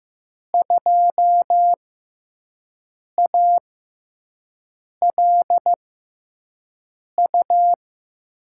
Morse code, heard as a keyed tone, about 15 wpm, spaced as for 5 wpm, 700 Hz. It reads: 2ALU
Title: USA Call Signs - 5 Characters - 6x Character Spacing - 15wpm